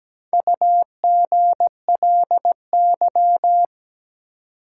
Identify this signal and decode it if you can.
Morse code, heard as a keyed tone, about 17 wpm, 700 Hz. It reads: UGLY
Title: Four Letter Words - 17wpm